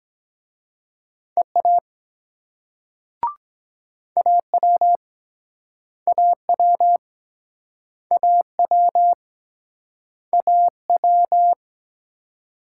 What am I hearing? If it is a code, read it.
Morse code, heard as a keyed tone, about 23 wpm, 700 Hz. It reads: EA AW AW AW AW